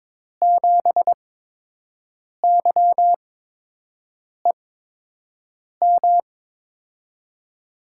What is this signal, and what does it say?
Morse code, heard as a keyed tone, about 22 wpm, spaced as for 6 wpm, 700 Hz. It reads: 7YEM
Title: USA Call Signs - 6 Characters - 8x Character Spacing - 22wpm